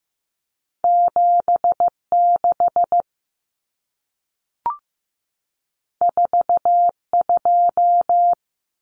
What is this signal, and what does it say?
Morse code, heard as a keyed tone, about 15 wpm, 700 Hz. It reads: 76 42
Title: Numbers - 2-Digit 15wpm